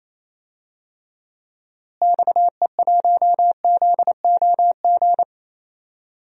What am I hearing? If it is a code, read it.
Morse code, heard as a keyed tone, about 27 wpm, 700 Hz. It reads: XE1ZOG
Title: DXCC-100 Call Signs - 6 Characters - Repeated 3x - 28wpm